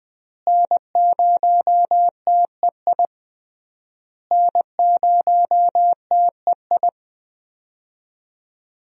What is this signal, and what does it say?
Morse code, heard as a keyed tone, about 20 wpm, 700 Hz. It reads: N0TEI N0TEI
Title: USA Call Signs - 5 Characters - Repeated 3x - 20wpm